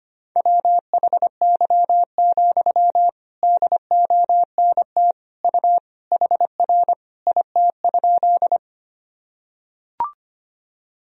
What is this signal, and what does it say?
Morse code, heard as a keyed tone, about 25 wpm, 700 Hz. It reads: WHY, DONT U HR IT?
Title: Sentences from Top 300 Words and Abbreviations 25wpm